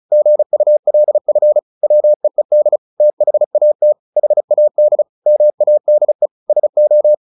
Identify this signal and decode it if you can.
Morse code, heard as a keyed tone, about 34 wpm, 600 Hz. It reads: GULF WEED THAT HAD MADE SO
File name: the old man and the sea / the old man and the sea 1 35wpm